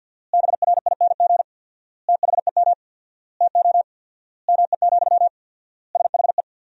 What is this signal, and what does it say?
Morse code, heard as a keyed tone, about 51 wpm, 700 Hz. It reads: BRING THEM TO ME, SHE
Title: 4x Word-Spacing - Sentences from Top 500 Words - 50wpm